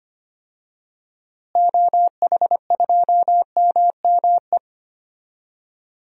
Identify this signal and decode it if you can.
Morse code, heard as a keyed tone, about 25 wpm, 700 Hz. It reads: OH2MME
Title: DXCC-100 Call Signs - 6 Characters - Repeated 3x - 25wpm